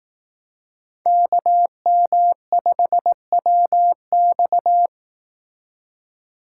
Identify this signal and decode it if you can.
Morse code, heard as a keyed tone, about 18 wpm, 700 Hz. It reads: KM5WX
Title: Speed Racing - USA 2x2 Call Signs 15wpm